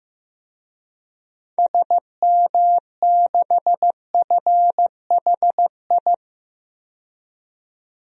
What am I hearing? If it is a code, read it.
Morse code, heard as a keyed tone, about 15 wpm, 700 Hz. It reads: SM6FHI
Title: DXCC-100 Call Signs - 6 Characters - 15wpm